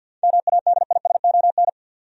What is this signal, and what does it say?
Morse code, heard as a keyed tone, about 52 wpm, 700 Hz. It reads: MADISON
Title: USA Common Names - 50wpm